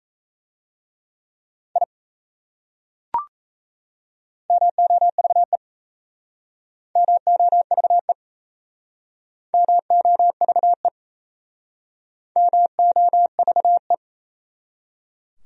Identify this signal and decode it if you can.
Morse code, heard as a keyed tone, about 35 wpm, 700 Hz. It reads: I MOVE MOVE MOVE MOVE
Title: Speed-Racing Top 500 Words 28wpm